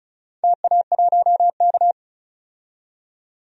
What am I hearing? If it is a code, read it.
Morse code, heard as a keyed tone, about 34 wpm, 700 Hz. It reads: TA1K
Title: DXCC-100 Call Signs - 4 Characters - 35wpm